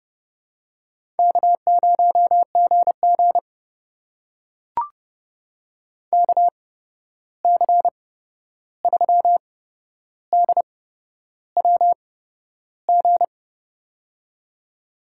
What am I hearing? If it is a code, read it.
Morse code, heard as a keyed tone, about 30 wpm, spaced as for 11 wpm, 700 Hz. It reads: K0GG KC3DWG
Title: USA Call Signs - 8x Character Spacing - 30wpm